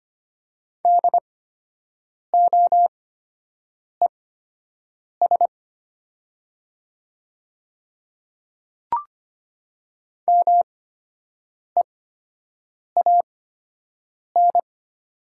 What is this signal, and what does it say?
Morse code, heard as a keyed tone, about 25 wpm, spaced as for 7 wpm, 700 Hz. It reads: DOES MEAN
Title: 8x Character Spacing - Top 200 Words - 25wpm